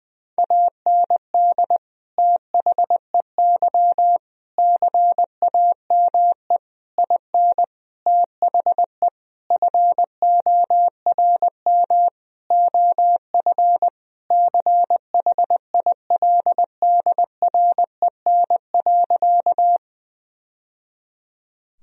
Morse code, heard as a keyed tone, about 21 wpm, 700 Hz. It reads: AND THEY CAME IN THE FORM OF CHILDREN.
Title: Sentences from Top 1000 Words - Review Part 1 - 20wpm